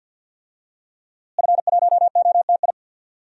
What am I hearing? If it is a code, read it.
Morse code, heard as a keyed tone, about 48 wpm, 700 Hz. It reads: F1OTI
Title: DXCC-100 Call Signs - 5 Characters - Repeated 2x - 50wpm